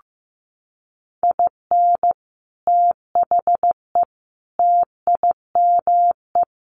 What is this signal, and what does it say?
Morse code, heard as a keyed tone, about 15 wpm, 700 Hz. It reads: IN THE TIME
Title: A Sound Of Thunder 15wpm